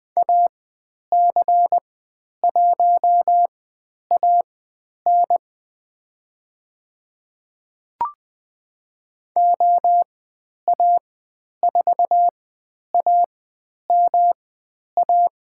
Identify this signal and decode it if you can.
Morse code, heard as a keyed tone, about 20 wpm, spaced as for 10 wpm, 700 Hz. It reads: AC1AN OA4AMA
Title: 109 Lesson 18g - Letter m - Call Signs 20wpm